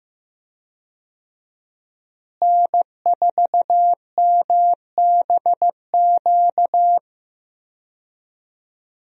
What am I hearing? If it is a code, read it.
Morse code, heard as a keyed tone, about 15 wpm, 700 Hz. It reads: N4MBQ
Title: USA Call Signs - 5 Characters - Repeated 3x - 15wpm